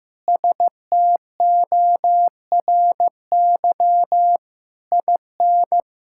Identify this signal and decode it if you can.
Morse code, heard as a keyed tone, about 14 wpm, 700 Hz. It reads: STORY IN
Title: Sentences from Top 1000 Words - Review Part 5 - 15wpm